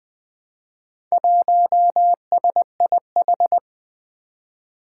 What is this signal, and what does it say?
Morse code, heard as a keyed tone, about 20 wpm, 700 Hz. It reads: 1SIH
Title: Four Challenging Letter-Number Combinations 20wpm